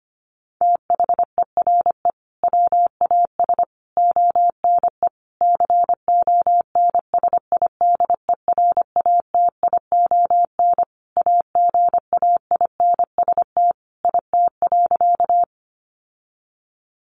Morse code, heard as a keyed tone, about 25 wpm, 700 Hz. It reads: THERE WAS ONE CONSIDERATION AGAINST IT.
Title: Sentences from Top 1500 Words 25wpm